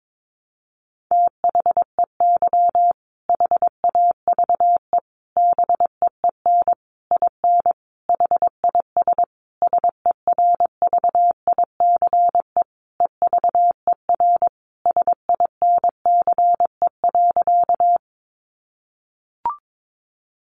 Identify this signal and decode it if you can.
Morse code, heard as a keyed tone, about 22 wpm, 700 Hz. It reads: THEY HAVE BEEN IN HIS SERVICE EVER SINCE.
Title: Sentences from Top 700 Words 22wpm